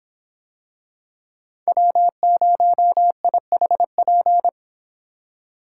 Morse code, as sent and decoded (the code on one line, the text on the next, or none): .-- ----- .. .... .--.
W0IHP